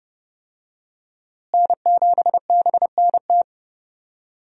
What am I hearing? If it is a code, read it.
Morse code, heard as a keyed tone, about 32 wpm, 700 Hz. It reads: N7BNT
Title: USA Call Signs - 5 Characters - Repeated 2x - 30wpm